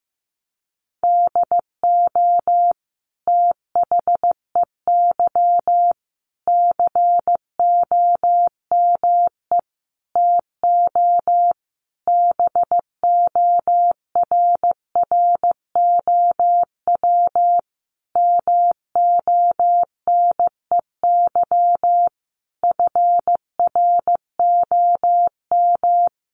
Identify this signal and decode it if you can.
Morse code, heard as a keyed tone, about 15 wpm, 700 Hz. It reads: DO THEY COME TO BORROW MONEY FROM
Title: Sentences from Top 1500 Words 15wpm